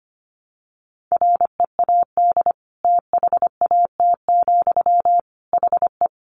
Morse code, heard as a keyed tone, about 25 wpm, 700 Hz. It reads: READ THAT, HE
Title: Sentences from Top 200 Words 25wpm